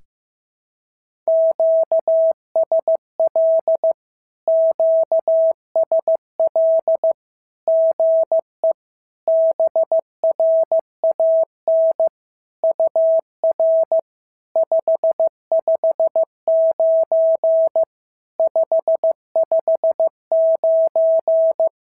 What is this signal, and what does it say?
Morse code, heard as a keyed tone, about 15 wpm, 650 Hz. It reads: QSL QSL GE BRAN UR 559 559
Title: SOTA QSO Practice 15wpm